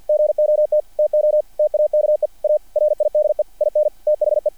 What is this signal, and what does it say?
Morse code, heard as a keyed tone, about 48 wpm, 600 Hz. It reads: GOT TO TAKE A RIDE IN THE
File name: archived / CRSnet